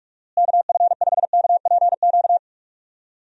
Morse code, heard as a keyed tone, about 46 wpm, 700 Hz. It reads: KF5KPQ
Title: USA Call Signs - 6 Characters - Repeated 2x - 45wpm